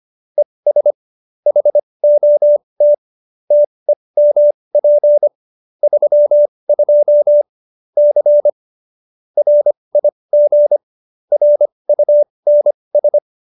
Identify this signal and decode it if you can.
Morse code, heard as a keyed tone, about 25 wpm, spaced as for 20 wpm, 590 Hz. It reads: ES HOT TEMP 32 C RIG RUNS
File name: qso216b 20